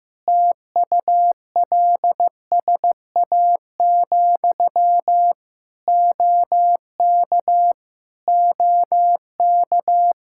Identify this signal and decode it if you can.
Morse code, heard as a keyed tone, about 15 wpm, 700 Hz. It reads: TULSA, OK OK
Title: Practice QSO - Exchange 1 - Beginning - 15wpm